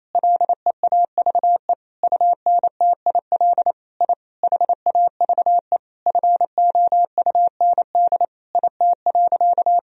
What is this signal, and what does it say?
Morse code, heard as a keyed tone, about 28 wpm, 700 Hz. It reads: LEAVE UNTIL I HAVE FOUND IT.
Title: Sentences from Top 1000 Words - Review Part 1 - 28wpm